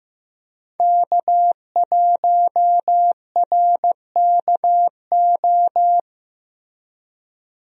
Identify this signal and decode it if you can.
Morse code, heard as a keyed tone, about 16 wpm, 700 Hz. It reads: K1RKO
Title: USA Call Signs - 5 Characters - Repeated 2x - 15wpm